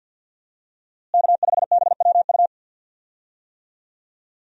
Morse code, heard as a keyed tone, about 51 wpm, 700 Hz. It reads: K5BWU
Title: USA Call Signs - 5 Characters - Repeated 2x - 50wpm